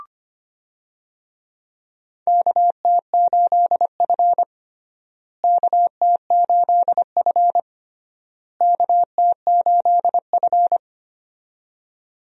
Morse code, heard as a keyed tone, about 25 wpm, 700 Hz. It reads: KT8F KT8F KT8F
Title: USA Call Signs - 4 Characters - Repeated 3x - 25wpm